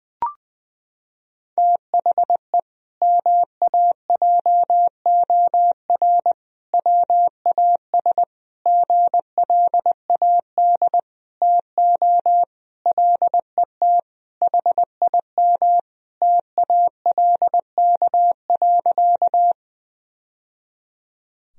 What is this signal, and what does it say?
Morse code, heard as a keyed tone, about 21 wpm, 700 Hz. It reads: THE MAJOR WAS GLAD TO LET HIM TALK.
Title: Sentences from Top 1000 Words - Review Part 4 - 20wpm